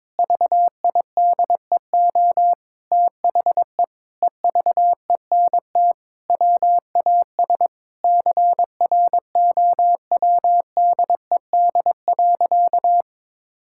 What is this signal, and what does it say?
Morse code, heard as a keyed tone, about 22 wpm, 700 Hz. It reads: VIDEO THE EVENT WAS CROWDED.
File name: Sentences with 7-Word Phrases - 22wpm